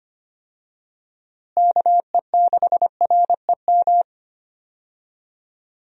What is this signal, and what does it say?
Morse code, heard as a keyed tone, about 25 wpm, 700 Hz. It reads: KE6REM
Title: USA Call Signs - 6 Characters - Repeated 2x - 25wpm